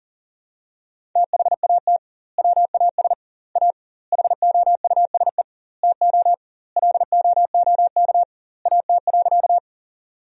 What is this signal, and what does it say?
Morse code, heard as a keyed tone, about 40 wpm, 700 Hz. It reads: THAT WAS A HOUSE TO LOOK AT.
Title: 2x Word-Spacing - Sentences from Top 200 Words - 40wpm